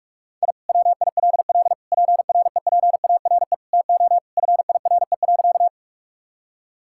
Morse code, heard as a keyed tone, about 46 wpm, 700 Hz. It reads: I WILL PREPARE TO FIRE.
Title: Sentences from Top 1000 Words - Review Part 6 - 45wpm